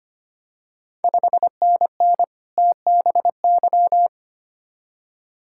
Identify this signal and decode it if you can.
Morse code, heard as a keyed tone, about 25 wpm, 700 Hz. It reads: 5NN TBY
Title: Ontario QSO Party Exchange 25wpm